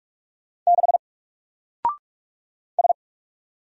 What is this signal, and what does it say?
Morse code, heard as a keyed tone, about 44 wpm, 700 Hz. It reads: 6 S
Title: Single Character (Letter:Number:Punctuation:Prosign) 45wpm